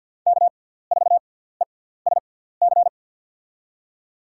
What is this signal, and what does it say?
Morse code, heard as a keyed tone, about 47 wpm, spaced as for 17 wpm, 700 Hz. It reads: K4ESC
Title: USA Call Signs - 5 Characters - 6x Character Spacing - 50wpm